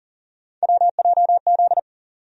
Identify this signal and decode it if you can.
Morse code, heard as a keyed tone, about 40 wpm, 700 Hz. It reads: WJZ
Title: USA Call Sign Postfixes - 3 Letters 40wpm